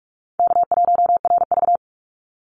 Morse code, heard as a keyed tone, about 44 wpm, 700 Hz. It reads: K1RV
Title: Call Signs Repeated 3x - 45wpm